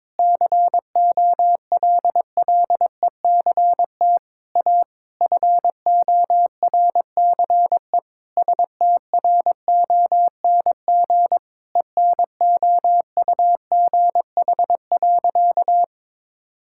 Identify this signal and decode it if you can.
Morse code, heard as a keyed tone, about 22 wpm, 700 Hz. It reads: COLLECT A FORCE STRONG ENOUGH.
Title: Sentences from Top 1600 Words 22wpm